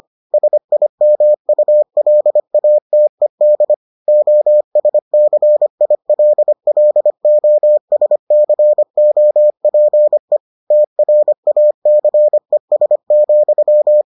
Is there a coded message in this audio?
Yes: Morse code from 0.3 to 14.0 s, heard as a keyed tone, about 25 wpm, 600 Hz. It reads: SIMULATED OSCILLOSCOPE TRACES,